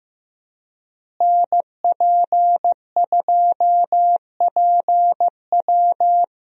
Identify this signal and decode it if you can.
Morse code, heard as a keyed tone, about 15 wpm, 700 Hz. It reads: NP2PW
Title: USA Call Signs - 5 Characters - Repeated 3x - 15wpm